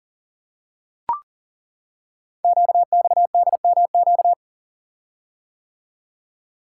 Morse code, heard as a keyed tone, about 38 wpm, 690 Hz. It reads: QXDMQ
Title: Code Groups - 5 Letters - 40wpm